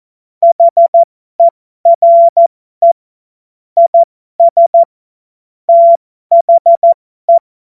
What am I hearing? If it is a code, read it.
Morse code, heard as a keyed tone, about 14 wpm, spaced as for 12 wpm, 680 Hz. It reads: HERE IS THE
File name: Morsekurs / Witze